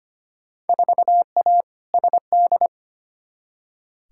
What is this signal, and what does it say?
Morse code, heard as a keyed tone, about 25 wpm, 700 Hz. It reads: 4A SD
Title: Field Day Exchanges - 25wpm